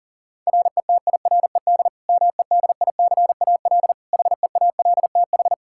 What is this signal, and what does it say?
Morse code, heard as a keyed tone, about 40 wpm, 700 Hz. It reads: RETIRED MEDICAL HEALTH
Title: Practice QSO - Exchange 3 - More Info - 40wpm